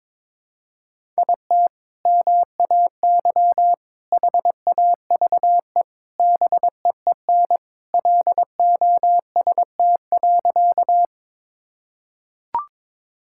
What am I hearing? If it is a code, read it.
Morse code, heard as a keyed tone, about 22 wpm, 700 Hz. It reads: IT MAY HAVE BEEN LOST.
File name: practice / 22wpm / Sentences with 3-Word Phrases - 22wpm